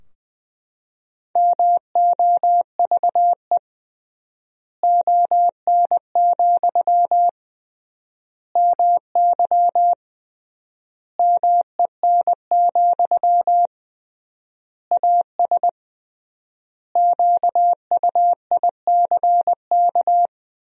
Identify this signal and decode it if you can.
Morse code, heard as a keyed tone, about 20 wpm, 700 Hz. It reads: MOVE ON, MY MEN, AS QUICK
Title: 3x Word-Spacing - Sentences from Top 500 Words - 20wpm